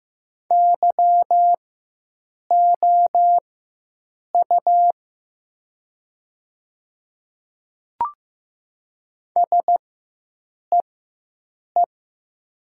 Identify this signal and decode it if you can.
Morse code, heard as a keyed tone, about 15 wpm, spaced as for 7 wpm, 700 Hz. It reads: YOU SEE
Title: 4x Character Spacing - Top 100 Words - 15wpm